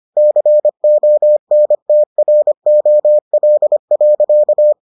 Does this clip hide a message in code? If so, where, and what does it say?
Yes: Morse code from 0.2 to 4.7 s, heard as a keyed tone, about 25 wpm, 600 Hz. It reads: CONTROL.